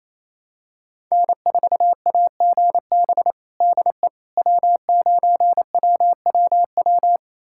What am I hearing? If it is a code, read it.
Morse code, heard as a keyed tone, about 28 wpm, 700 Hz. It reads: N4AGB DE W9WWW
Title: USA Callsign DE Callsign - 5 Characters - 28wpm